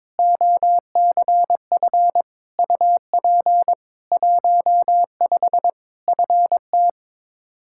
Morse code, heard as a keyed tone, about 22 wpm, 700 Hz. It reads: OCF UP 15 FT
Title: K1USN SST QSO Practice 22wpm